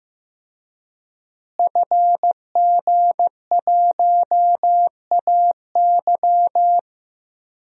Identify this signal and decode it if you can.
Morse code, heard as a keyed tone, about 16 wpm, 700 Hz. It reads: FG1AY
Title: DXCC-100 Call Signs - 5 Characters - Repeated 3x - 15wpm